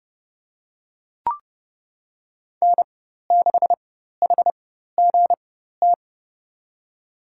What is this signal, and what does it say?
Morse code, heard as a keyed tone, about 30 wpm, spaced as for 14 wpm, 700 Hz. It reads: N6HGT
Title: USA Call Signs - 5 Characters - 4x Character Spacing - 30wpm